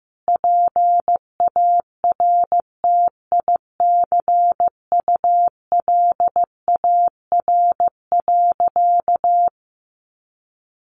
Morse code, heard as a keyed tone, about 15 wpm, 700 Hz. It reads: PARTICULAR.
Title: Sentences from Top 900 Words 15wpm